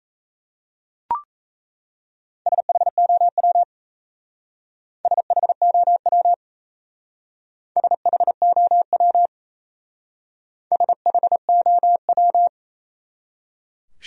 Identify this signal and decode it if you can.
Morse code, heard as a keyed tone, about 37 wpm, 700 Hz. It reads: SHOW SHOW SHOW SHOW